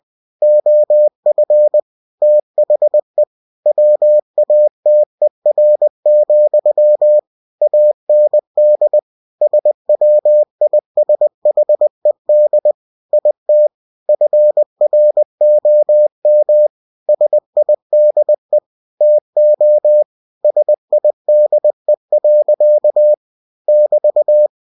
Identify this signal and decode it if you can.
Morse code, heard as a keyed tone, about 20 wpm, 600 Hz. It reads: OF THE WATER, AND SWISHED IT FROM SIDE TO SIDE. <BT>
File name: cw-books / winnie-the-pooh / winnie-20wpm-page-0088